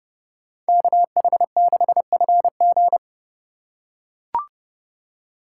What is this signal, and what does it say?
Morse code, heard as a keyed tone, about 30 wpm, 700 Hz. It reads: KH6FG